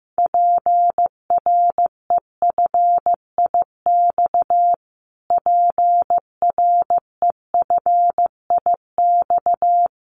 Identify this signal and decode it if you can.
Morse code, heard as a keyed tone, about 15 wpm, 700 Hz. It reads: PREFIX PREFIX
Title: Callsign Prefixes for Top 25 DXCC - Repeated 3x - 15wpm